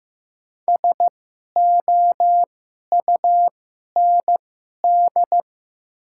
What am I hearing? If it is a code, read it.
Morse code, heard as a keyed tone, about 15 wpm, spaced as for 11 wpm, 700 Hz. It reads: SOUND
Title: 2x Character Spacing - Top 500 Words - 15wpm